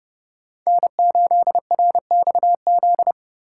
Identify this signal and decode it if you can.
Morse code, heard as a keyed tone, about 30 wpm, 700 Hz. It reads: N8RXZ